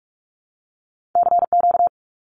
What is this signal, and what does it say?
Morse code, heard as a keyed tone, about 46 wpm, 700 Hz. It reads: CQ
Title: Abbreviations Repeated 3x - 45wpm